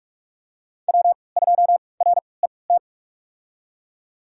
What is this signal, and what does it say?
Morse code, heard as a keyed tone, about 48 wpm, spaced as for 26 wpm, 700 Hz. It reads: W2RET
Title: USA Call Signs - 5 Characters - 3x Character Spacing - 45wpm